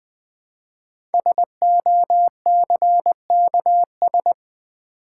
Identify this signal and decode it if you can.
Morse code, heard as a keyed tone, about 20 wpm, 700 Hz. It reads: SOCKS
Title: Easy Words with Hints - 20wpm